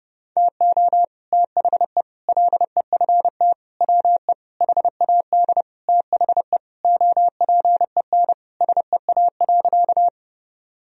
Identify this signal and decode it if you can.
Morse code, heard as a keyed tone, about 30 wpm, 700 Hz. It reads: TO THE LEFT WE HAD THE OPEN SEA.
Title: Sentences from Top 1000 Words - Review Part 1 - 30wpm